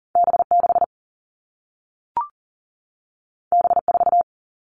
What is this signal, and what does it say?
Morse code, heard as a keyed tone, about 40 wpm, 700 Hz. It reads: B6 B4
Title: All Cut Numbers - 2-Digit 40wpm